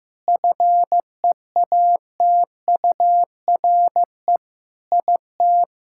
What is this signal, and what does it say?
Morse code, heard as a keyed tone, about 15 wpm, 700 Hz. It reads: FEATURE IT
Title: Sentences from Top 1000 Words - Review Part 5 - 15wpm